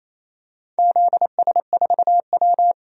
Morse code, heard as a keyed tone, about 28 wpm, 700 Hz. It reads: ZS4W
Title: DXCC-100 Call Signs - 4 Characters - 28wpm